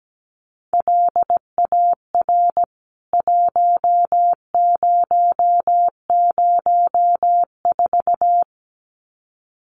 Morse code, heard as a keyed tone, about 17 wpm, 700 Hz. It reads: LAR 1004
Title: CWT Exchange 17wpm